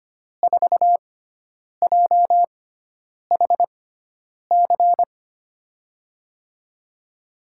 Morse code, heard as a keyed tone, about 25 wpm, spaced as for 9 wpm, 700 Hz. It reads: 4JHC
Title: USA Call Signs - 6x Character Spacing - 25wpm